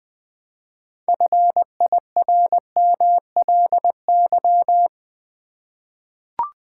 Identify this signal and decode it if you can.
Morse code, heard as a keyed tone, about 20 wpm, 700 Hz. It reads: FIRMLY
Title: 6 Letter Words - First 3 Letters Given - 20wpm